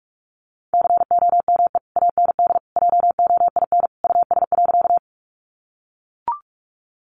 Morse code, heard as a keyed tone, about 44 wpm, 700 Hz. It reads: COME AND JOIN US.
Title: Sentences from Top 1400 Words 45wpm